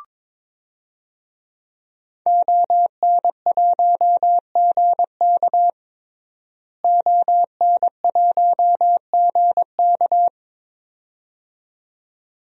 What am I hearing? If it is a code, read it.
Morse code, heard as a keyed tone, about 22 wpm, 700 Hz. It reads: ON1GK ON1GK
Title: DXCC-100 Call Signs - 5 Characters - Repeated 2x - 22wpm